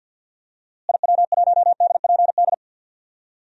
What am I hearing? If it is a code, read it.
Morse code, heard as a keyed tone, about 49 wpm, 700 Hz. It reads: IW1DPD
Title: DXCC-100 Call Signs - 6 Characters - Repeated 2x - 50wpm